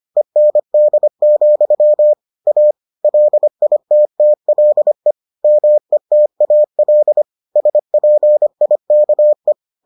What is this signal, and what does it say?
Morse code, heard as a keyed tone, about 25 wpm, 600 Hz. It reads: END, A LITTLE METAL SPIKE